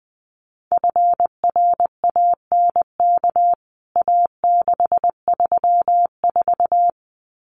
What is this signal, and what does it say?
Morse code, heard as a keyed tone, about 20 wpm, 700 Hz. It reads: FRANK A634
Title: CWT Exchange 20wpm